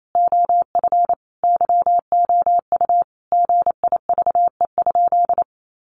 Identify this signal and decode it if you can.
Morse code, heard as a keyed tone, about 28 wpm, 700 Hz. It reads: OF YOU GIVE?
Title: Sentences from Top 200 Words 28wpm